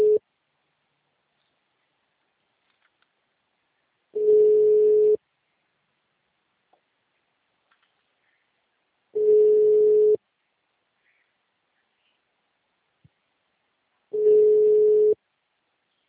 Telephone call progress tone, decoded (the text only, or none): ringback tone